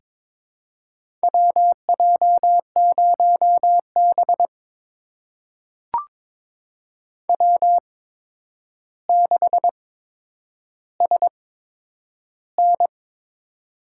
Morse code, heard as a keyed tone, about 22 wpm, spaced as for 9 wpm, 700 Hz. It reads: WJ0B W6SN
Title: USA Call Signs - 4 Characters - 8x Character Spacing - 22wpm